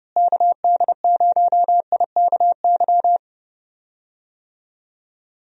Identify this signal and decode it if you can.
Morse code, heard as a keyed tone, about 30 wpm, 700 Hz. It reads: KD0IKY
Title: USA Call Signs - 6 Characters - Repeated 3x - 30wpm